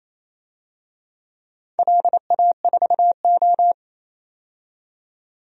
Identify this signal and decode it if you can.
Morse code, heard as a keyed tone, about 28 wpm, 700 Hz. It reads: LA4O
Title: DXCC-100 Call Signs - 4 Characters - 28wpm